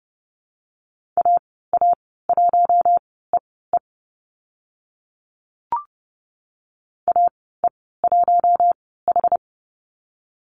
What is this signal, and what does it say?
Morse code, heard as a keyed tone, about 30 wpm, spaced as for 17 wpm, 700 Hz. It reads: AA1EE AE1H
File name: Lesson 14g - Number 2 - Call Signs 30wpm